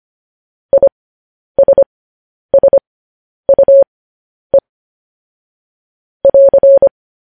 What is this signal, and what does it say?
Morse code, heard as a keyed tone, about 25 wpm, spaced as for 10 wpm, 550 Hz. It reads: ISSUE <AR>